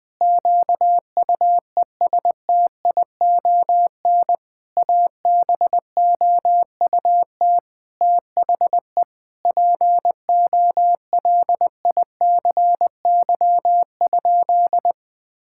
Morse code, heard as a keyed tone, about 21 wpm, 700 Hz. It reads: QUESTION ABOUT THE POLICY?
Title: Sentences with 6-Word Phrases - 20wpm